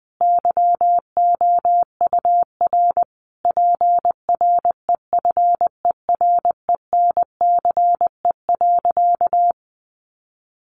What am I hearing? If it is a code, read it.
Morse code, heard as a keyed tone, about 20 wpm, 700 Hz. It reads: YOUR PREFERENCE.
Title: Sentences from Top 1500 Words 20wpm